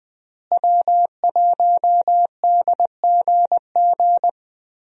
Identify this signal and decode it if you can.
Morse code, heard as a keyed tone, about 20 wpm, 700 Hz. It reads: W1DGG
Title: USA Call Signs - 5 Characters - Repeated 3x - 20wpm